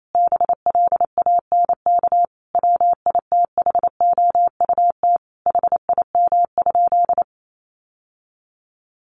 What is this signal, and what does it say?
Morse code, heard as a keyed tone, about 28 wpm, 700 Hz. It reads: BLANK WITHOUT HIM?
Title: Sentences from Top 1500 Words 28wpm